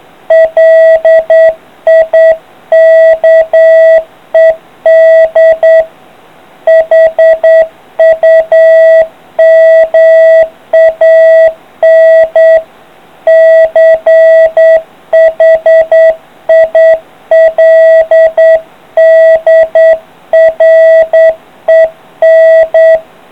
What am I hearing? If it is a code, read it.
Morse code, heard as a keyed tone, about 9 wpm, 650 Hz. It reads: LIKED HUMAN CHILDREN